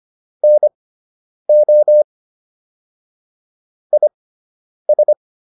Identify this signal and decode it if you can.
Morse code, heard as a keyed tone, about 25 wpm, spaced as for 9 wpm, 600 Hz. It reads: NO IS